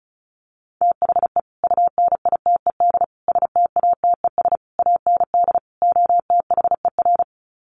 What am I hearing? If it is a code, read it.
Morse code, heard as a keyed tone, about 34 wpm, 700 Hz. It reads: THE UNITED STATES AND OTHER
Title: Sets of 5 Words Encore 35wpm